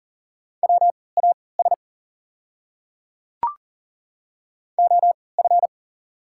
Morse code, heard as a keyed tone, about 39 wpm, spaced as for 23 wpm, 690 Hz. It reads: WAS OF
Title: 3x Character Spacing - Top 200 Words - 40wpm